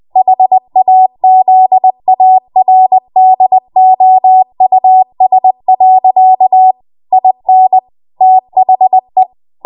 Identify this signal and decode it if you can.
Morse code, heard as a keyed tone, about 20 wpm, 750 Hz. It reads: HAZARDOUS. IN THE
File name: Morse / 20WPM